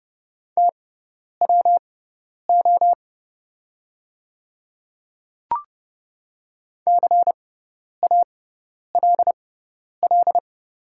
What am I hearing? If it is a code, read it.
Morse code, heard as a keyed tone, about 30 wpm, spaced as for 10 wpm, 700 Hz. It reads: TWO CALL
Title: 6x Character Spacing - Top 200 Words - 30wpm